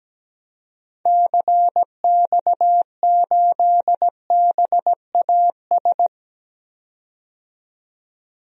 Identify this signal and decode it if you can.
Morse code, heard as a keyed tone, about 17 wpm, 700 Hz. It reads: CX8BAS
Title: DXCC-100 Call Signs - 6 Characters - 17wpm